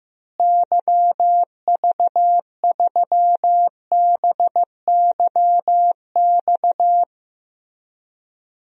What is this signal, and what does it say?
Morse code, heard as a keyed tone, about 16 wpm, 700 Hz. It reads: YV3BYX
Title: DXCC-100 Call Signs - 6 Characters - Repeated 2x - 15wpm